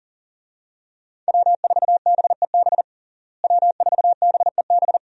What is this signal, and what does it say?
Morse code, heard as a keyed tone, about 40 wpm, 700 Hz. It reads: W4BEB W4BEB
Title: USA Call Signs - 5 Characters - Repeated 2x - 40wpm